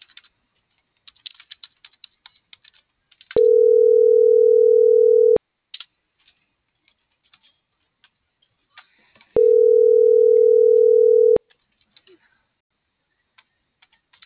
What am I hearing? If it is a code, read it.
Ringback tone (North America), 440 + 480 Hz.